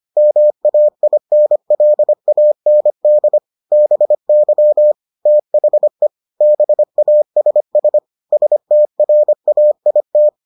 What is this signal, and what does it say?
Morse code, heard as a keyed tone, about 25 wpm, 600 Hz. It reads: MAINLAND BY THE BASS STRAIT